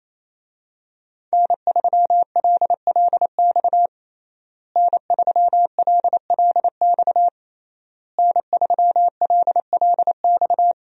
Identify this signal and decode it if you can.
Morse code, heard as a keyed tone, about 27 wpm, 700 Hz. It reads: N3LLX N3LLX N3LLX
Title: USA Call Signs - 5 Characters - Repeated 3x - 28wpm